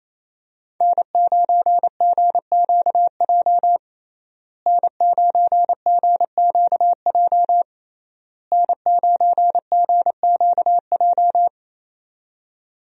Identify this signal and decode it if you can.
Morse code, heard as a keyed tone, about 28 wpm, 700 Hz. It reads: N9GQJ N9GQJ N9GQJ
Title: USA Call Signs - 5 Characters - Repeated 3x - 28wpm